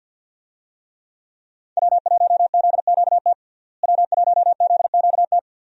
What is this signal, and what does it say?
Morse code, heard as a keyed tone, about 46 wpm, 700 Hz. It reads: W1ZQT W1ZQT
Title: USA Call Signs - 5 Characters - Repeated 2x - 50wpm